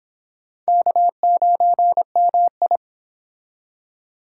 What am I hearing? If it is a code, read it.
Morse code, heard as a keyed tone, about 29 wpm, 700 Hz. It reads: K9MI